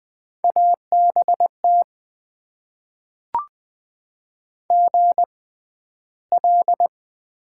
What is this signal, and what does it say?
Morse code, heard as a keyed tone, about 20 wpm, spaced as for 12 wpm, 700 Hz. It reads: ABT GL
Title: Top 90 QSO Elements - 6x Character Spacing - 20wpm